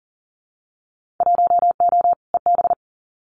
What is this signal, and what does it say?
Morse code, heard as a keyed tone, about 40 wpm, 700 Hz. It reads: 1O EB
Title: Winter Field Day Exchanges 40wpm